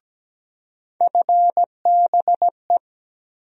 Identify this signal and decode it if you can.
Morse code, heard as a keyed tone, about 17 wpm, 700 Hz. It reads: FBE
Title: USA Call Sign Postfixes - 3 Letters 17wpm